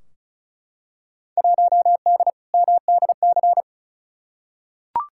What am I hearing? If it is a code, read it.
Morse code, heard as a keyed tone, about 36 wpm, 700 Hz. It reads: 1D MDC
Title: Field Day Exchanges - 35wpm